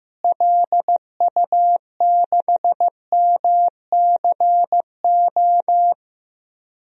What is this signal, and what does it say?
Morse code, heard as a keyed tone, about 15 wpm, 700 Hz. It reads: LU6MCO
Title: DXCC-100 Call Signs - 6 Characters - Repeated 2x - 15wpm